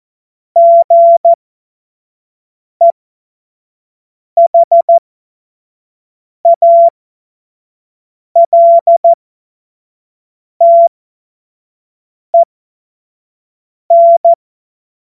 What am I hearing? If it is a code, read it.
Morse code, heard as a keyed tone, about 14 wpm, spaced as for 5 wpm, 680 Hz. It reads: GEHALTEN
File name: Morsekurs / Witze / Witz 20. Januar 2026 25 F70